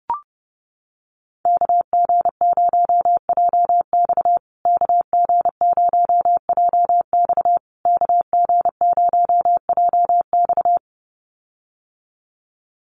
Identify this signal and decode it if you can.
Morse code, heard as a keyed tone, about 30 wpm, 700 Hz. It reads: KG0JX KG0JX KG0JX
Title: USA Call Signs Repeated 3x 30wpm